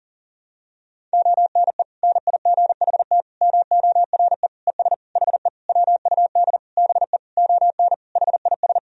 Morse code, heard as a keyed tone, about 40 wpm, 700 Hz. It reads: ONE NIGHT MORE ES HE WUD BE ON HIS
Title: Sentences from Top 300 Words and Abbreviations 40wpm